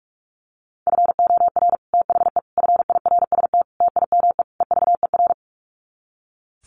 Morse code, heard as a keyed tone, about 44 wpm, 700 Hz. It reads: FOR THE FIRST TIME EVER
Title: Sets of 5 Words Encore2 45wpm